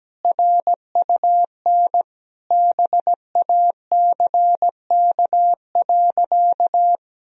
Morse code, heard as a keyed tone, about 17 wpm, 700 Hz. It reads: RUN BACK.
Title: Sentences from Top 1000 Words - Review Part 1 - 17wpm